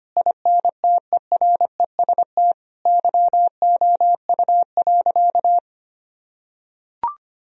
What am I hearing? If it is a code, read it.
Morse code, heard as a keyed tone, about 25 wpm, 700 Hz. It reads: INTEREST YOU.
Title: Sentences from Top 1000 Words - Review Part 4 - 25wpm